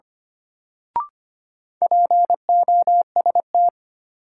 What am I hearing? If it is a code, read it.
Morse code, heard as a keyed tone, about 25 wpm, 700 Hz. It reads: POST
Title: Sentences from Top 1000 Words - New Word Drill - 25wpm